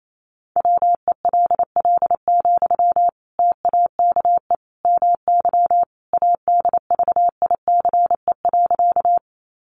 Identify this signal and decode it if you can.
Morse code, heard as a keyed tone, about 28 wpm, 700 Hz. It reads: WELL, TAKE MY ADVICE.
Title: Sentences from Top 1100 Words 28wpm